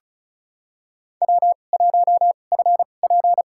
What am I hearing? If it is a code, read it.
Morse code, heard as a keyed tone, about 36 wpm, spaced as for 25 wpm, 700 Hz. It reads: W1FP
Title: USA Call Signs - 4 Characters - 2x Character Spacing - 35wpm